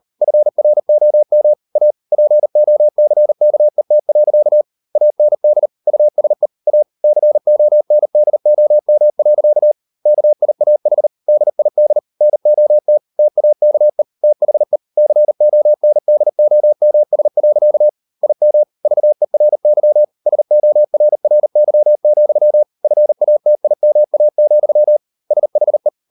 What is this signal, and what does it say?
Morse code, heard as a keyed tone, about 38 wpm, 600 Hz. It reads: FROM A POCKET. AND USE A CONDOM. KIAH DID NOT TAKE THE CONDOMS. IM VERY SORRY, FATIMA, SHE